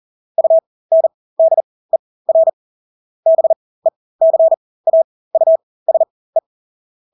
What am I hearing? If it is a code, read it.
Morse code, heard as a keyed tone, about 40 wpm, spaced as for 20 wpm, 650 Hz. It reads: UNDER BECAUSE
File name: Engelske ord 40-20 wpm